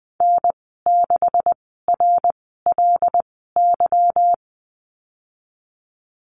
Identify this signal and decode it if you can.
Morse code, heard as a keyed tone, about 20 wpm, spaced as for 15 wpm, 700 Hz. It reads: N6RLY
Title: 200 Lesson 32g - Letter k - Call Signs 20wpm